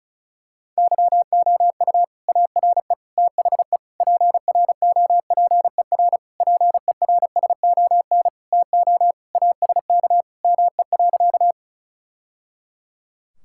Morse code, heard as a keyed tone, about 36 wpm, 700 Hz. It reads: YOU ARE THE PROPER PERSON TO ASK ME.
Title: Sentences from Top 1000 Words - Review Part 5 - 35wpm